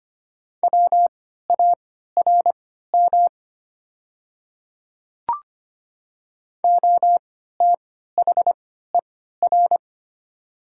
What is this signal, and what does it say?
Morse code, heard as a keyed tone, about 25 wpm, spaced as for 14 wpm, 700 Hz. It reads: WARM OTHER
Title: 3x Character Spacing - Top 500 Words - 25wpm